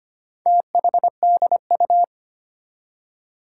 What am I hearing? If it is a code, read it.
Morse code, heard as a keyed tone, about 25 wpm, 700 Hz. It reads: THDU
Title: Four Challenging Letter-Number Combinations 25wpm